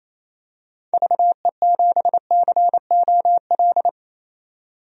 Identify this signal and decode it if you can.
Morse code, heard as a keyed tone, about 28 wpm, 700 Hz. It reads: VE7COL
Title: DXCC-100 Call Signs - 6 Characters - 28wpm